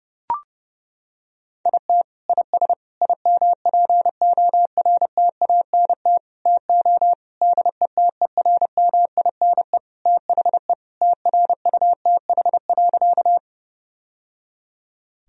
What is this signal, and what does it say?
Morse code, heard as a keyed tone, about 32 wpm, 700 Hz. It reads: IT IS IMPORTANT TO DETERMINE THE TRUTH.
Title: Sentences from Top 1000 Words - Review Part 4 - 30wpm